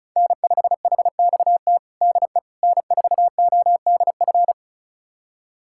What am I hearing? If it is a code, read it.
Morse code, heard as a keyed tone, about 36 wpm, 700 Hz. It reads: N5HXT DE N4ODF
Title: USA Callsign DE Callsign - 5 Characters - 35wpm